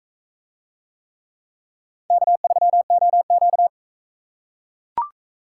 Morse code, heard as a keyed tone, about 44 wpm, 700 Hz. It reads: K3OQ